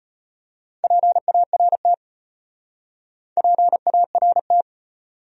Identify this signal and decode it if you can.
Morse code, heard as a keyed tone, about 34 wpm, 700 Hz. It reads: PART PART